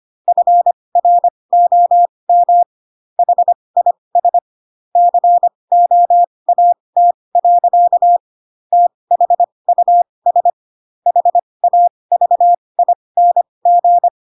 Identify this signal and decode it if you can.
Morse code, heard as a keyed tone, about 25 wpm, spaced as for 20 wpm, 700 Hz. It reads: FROM HIS COAT. THUS HAVING
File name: Pinocchio / Pinocchio ch2 15 25 ik1hge